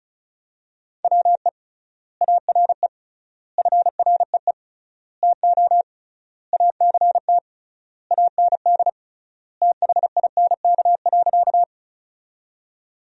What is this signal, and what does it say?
Morse code, heard as a keyed tone, about 34 wpm, 700 Hz. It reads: WE ARE FREE TO ACT AND THINK.
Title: 3x Word-Spacing - Sentences from Top 500 Words - 35wpm